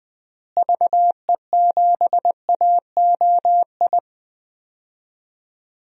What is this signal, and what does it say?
Morse code, heard as a keyed tone, about 20 wpm, 700 Hz. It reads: VE7AOI
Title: DXCC-100 Call Signs - 6 Characters - Repeated 3x - 20wpm